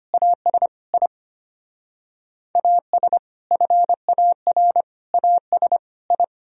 Morse code, heard as a keyed tone, about 33 wpm, 700 Hz. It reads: AS I AS FAR AS I